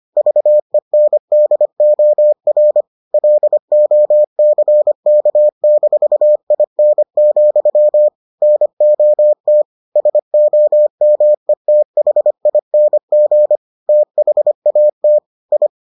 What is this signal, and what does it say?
Morse code, heard as a keyed tone, about 25 wpm, 600 Hz. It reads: VENDOR LOCK-IN, NOT SOMETHING THAT I